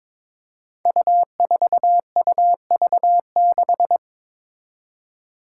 Morse code, heard as a keyed tone, about 22 wpm, 700 Hz. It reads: U4UV6
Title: Five Challenging Letter-Number Combinations 22wpm